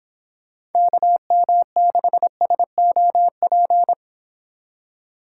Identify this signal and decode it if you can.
Morse code, heard as a keyed tone, about 30 wpm, 700 Hz. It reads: KM6SOP